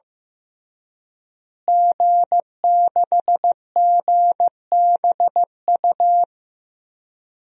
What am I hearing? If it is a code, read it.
Morse code, heard as a keyed tone, about 15 wpm, 700 Hz. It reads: G6GBU
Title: DXCC-100 Call Signs - 5 Characters - Repeated 3x - 15wpm